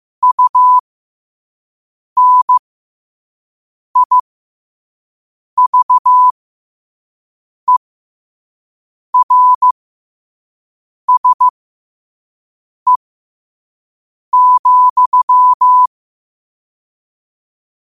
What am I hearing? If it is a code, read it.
Morse code, heard as a keyed tone, about 15 wpm, spaced as for 5 wpm, 1000 Hz. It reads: UNIVERSE,
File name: qotdfiles / 5wpm